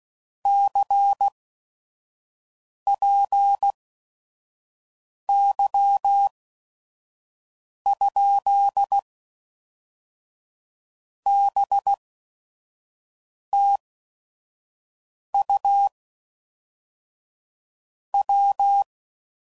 Morse code, heard as a keyed tone, about 16 wpm, spaced as for 5 wpm, 800 Hz. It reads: CPY? BTU W
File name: CW Code Groups @5WPM / QSOs / QSO WA7QOX 1